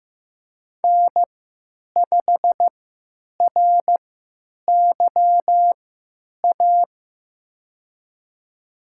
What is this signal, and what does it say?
Morse code, heard as a keyed tone, about 15 wpm, spaced as for 9 wpm, 700 Hz. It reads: N5RYA
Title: USA Call Signs - 3x Character Spacing - 15wpm